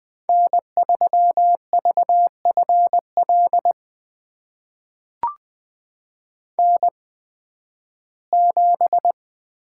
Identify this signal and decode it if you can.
Morse code, heard as a keyed tone, about 20 wpm, spaced as for 13 wpm, 700 Hz. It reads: N3VFL N7
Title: USA Call Signs - 5 Characters - 8x Character Spacing - 20wpm